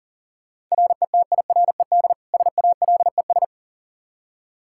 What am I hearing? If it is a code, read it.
Morse code, heard as a keyed tone, about 40 wpm, 700 Hz. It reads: RETIRED SALES
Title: Practice QSO - Exchange 3 - More Info - 40wpm